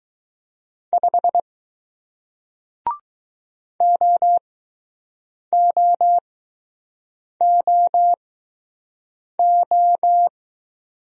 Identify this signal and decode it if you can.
Morse code, heard as a keyed tone, about 20 wpm, spaced as for 6 wpm, 700 Hz. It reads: 5 OOOO